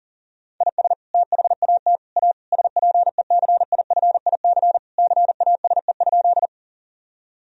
Morse code, heard as a keyed tone, about 40 wpm, 700 Hz. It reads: IS THAT A SPECIFIC CASE?
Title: Sentences from Top 1000 Words - Review Part 3 - 40wpm